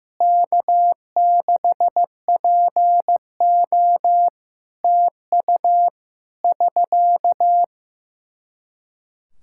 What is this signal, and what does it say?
Morse code, heard as a keyed tone, about 15 wpm, 700 Hz. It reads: K6PO TU <SK>
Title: Practice QSO - Exchange 4 - Ending - 15wpm